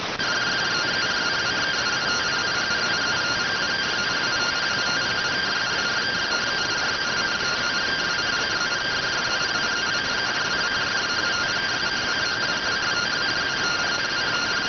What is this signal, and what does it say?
Two-tone teleprinter signal, about 170 Hz shift.